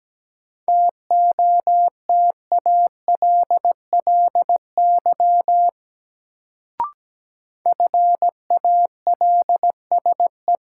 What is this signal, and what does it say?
Morse code, heard as a keyed tone, about 17 wpm, 700 Hz. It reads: TOTALLY FALSE
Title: Sentences from Top 1600 Words - New Word Drill - 17wpm